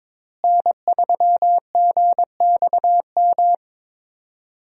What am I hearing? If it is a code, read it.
Morse code, heard as a keyed tone, about 22 wpm, 700 Hz. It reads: N3GXM
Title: USA Call Signs - 5 Characters - Repeated 2x - 22wpm